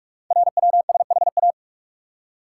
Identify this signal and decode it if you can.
Morse code, heard as a keyed tone, about 44 wpm, 690 Hz. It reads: RWSHA
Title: Code Groups - 5 Letters - 45wpm